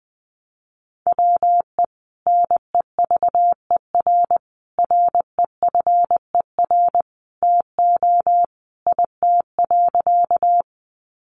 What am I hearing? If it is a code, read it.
Morse code, heard as a keyed tone, about 20 wpm, 700 Hz. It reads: WE NEVER REFER TO IT.
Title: Sentences from Top 1000 Words 20wpm